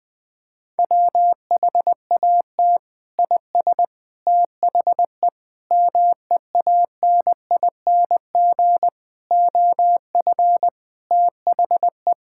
Morse code, heard as a keyed tone, about 20 wpm, 700 Hz. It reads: WHAT IS THE MEANING OF THE
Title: Sentences from Top 1600 Words 20wpm